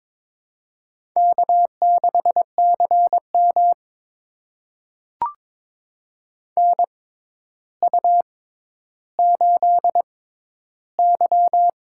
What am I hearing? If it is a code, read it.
Morse code, heard as a keyed tone, about 22 wpm, spaced as for 11 wpm, 700 Hz. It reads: K6CM NU8Y
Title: USA Call Signs - 4 Characters - 6x Character Spacing - 22wpm